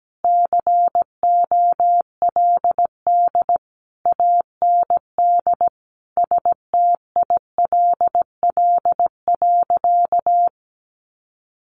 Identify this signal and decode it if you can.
Morse code, heard as a keyed tone, about 17 wpm, 700 Hz. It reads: COLD AND STILL.
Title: Sentences from Top 400 Words 17wpm